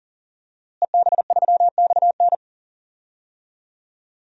Morse code, heard as a keyed tone, about 40 wpm, 700 Hz. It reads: EB3XN
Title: DXCC-100 Call Signs - 5 Characters - Repeated 3x - 40wpm